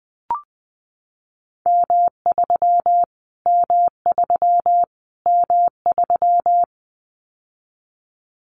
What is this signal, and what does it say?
Morse code, heard as a keyed tone, about 20 wpm, 700 Hz. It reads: M3 M3 M3
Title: Callsign Prefixes for Top 25 DXCC - Repeated 3x - 20wpm